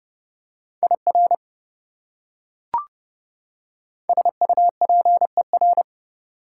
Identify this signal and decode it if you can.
Morse code, heard as a keyed tone, about 32 wpm, 700 Hz. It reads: IR SUPER
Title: Prefixes and Suffixes 30wpm